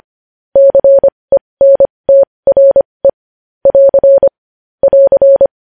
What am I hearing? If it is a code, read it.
Morse code, heard as a keyed tone, about 25 wpm, spaced as for 20 wpm, 550 Hz. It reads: CENTRE <AR> <AR>
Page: News Headlines in Morse Code at 20 WPM